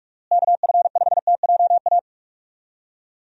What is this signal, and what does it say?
Morse code, heard as a keyed tone, about 41 wpm, 700 Hz. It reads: KF5TJA
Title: USA Call Signs - 6 Characters - Repeated 2x - 45wpm